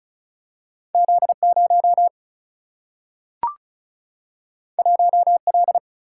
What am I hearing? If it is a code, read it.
Morse code, heard as a keyed tone, about 36 wpm, 700 Hz. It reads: Z0 1L
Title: Single-Letter Single-Number 35wpm